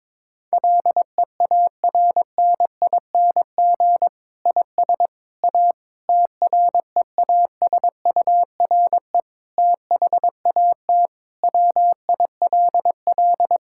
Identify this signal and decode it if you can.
Morse code, heard as a keyed tone, about 21 wpm, 700 Hz. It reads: LEARNING IS A TREASURE THAT WILL
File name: Difficult Sentences with Hints - 22wpm